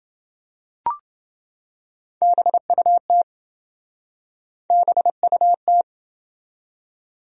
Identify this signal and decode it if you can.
Morse code, heard as a keyed tone, about 27 wpm, 700 Hz. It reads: BUT BUT